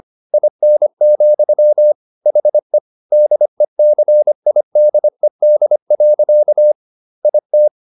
Morse code, heard as a keyed tone, about 25 wpm, 600 Hz. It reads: IN, HE DECIDED. IT